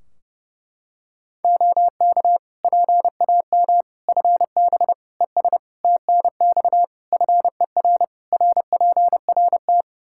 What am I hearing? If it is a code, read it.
Morse code, heard as a keyed tone, about 30 wpm, 700 Hz. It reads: OK PAM FB ES TNX FER RPRT
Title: K1USN SST QSO Practice 30wpm